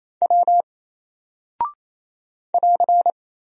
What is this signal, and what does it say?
Morse code, heard as a keyed tone, about 27 wpm, 700 Hz. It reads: W <AR>
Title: Single Character (Letter:Number:Punctuation:Prosign) 28wpm